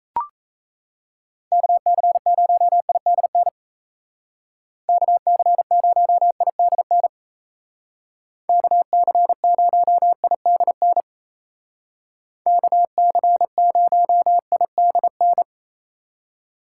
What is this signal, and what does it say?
Morse code, heard as a keyed tone, about 37 wpm, 700 Hz. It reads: KC0IDN KC0IDN KC0IDN KC0IDN